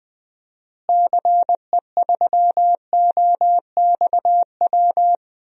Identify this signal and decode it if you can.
Morse code, heard as a keyed tone, about 20 wpm, 700 Hz. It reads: CE3OXW
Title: DXCC-100 Call Signs - 6 Characters - Repeated 3x - 20wpm